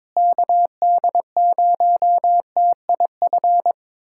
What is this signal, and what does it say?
Morse code, heard as a keyed tone, about 22 wpm, 700 Hz. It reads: KD0TIF
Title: USA Callsign DE Callsign - 6 Characters - 22wpm